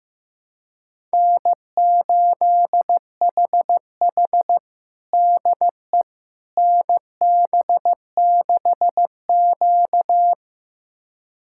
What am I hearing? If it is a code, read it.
Morse code, heard as a keyed tone, about 15 wpm, 700 Hz. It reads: N8HH DE NB6Q
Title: USA Callsign DE Callsign - 4 Characters - 15wpm